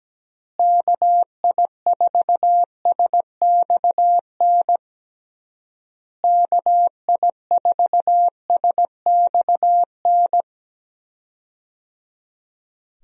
Morse code, heard as a keyed tone, about 17 wpm, 700 Hz. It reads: KI4SXN KI4SXN
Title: USA Call Signs - 6 Characters - Repeated 3x - 17wpm